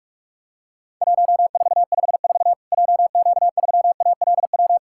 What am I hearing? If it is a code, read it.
Morse code, heard as a keyed tone, about 46 wpm, 700 Hz. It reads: 1454 JQ3ALW
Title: Worked All Europe DX Contest - QTC Exchange - 45wpm